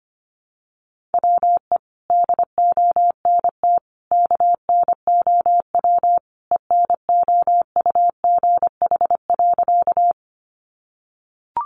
Morse code, heard as a keyed tone, about 25 wpm, 700 Hz. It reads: WE DONT KNOW ENOUGH.
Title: Sentences from Top 400 Words 25wpm